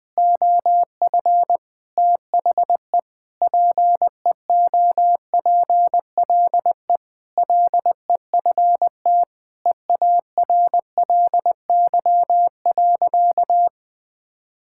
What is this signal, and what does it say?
Morse code, heard as a keyed tone, about 21 wpm, 700 Hz. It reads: OF THE PEOPLE LEFT EARLY.
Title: Sentences with 3-Word Phrases - Spoken Beginning - 20wpm